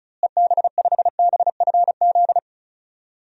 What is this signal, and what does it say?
Morse code, heard as a keyed tone, about 34 wpm, 700 Hz. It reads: EB5BFZ
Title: DXCC-100 Call Signs - 6 Characters - Repeated 2x - 35wpm